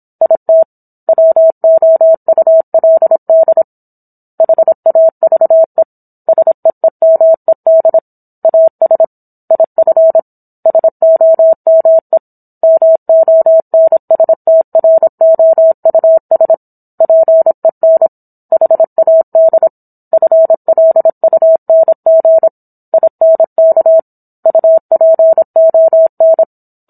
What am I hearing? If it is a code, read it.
Morse code, heard as a keyed tone, about 27 wpm, 650 Hz. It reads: IT WOULD HAVE SEEMED AS IF SOME MONSTROUS PEN HAD FLUNG INK UPON